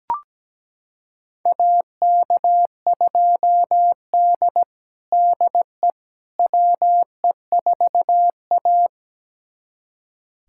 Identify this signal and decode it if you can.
Morse code, heard as a keyed tone, about 17 wpm, 700 Hz. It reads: AK2D DE WE4A
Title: USA Callsign DE Callsign - 4 Characters - 17wpm